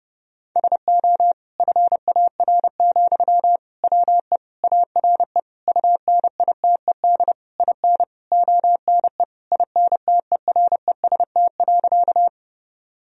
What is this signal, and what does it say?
Morse code, heard as a keyed tone, about 30 wpm, 700 Hz. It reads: SO FAR, WE ARE UNITED IN ONE INTEREST.
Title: Sentences from Top 1000 Words - Review Part 4 - 30wpm